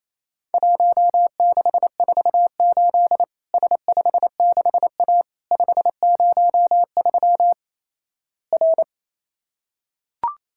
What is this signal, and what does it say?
Morse code, heard as a keyed tone, about 28 wpm, 700 Hz. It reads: 1648 S56A 503 R
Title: Worked All Europe DX Contest - QTC Exchange - 28wpm